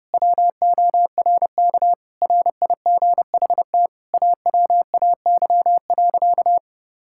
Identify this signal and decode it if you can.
Morse code, heard as a keyed tone, about 30 wpm, 700 Hz. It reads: WORK RIGHT AWAY.
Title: Sentences from Top 1600 Words 30wpm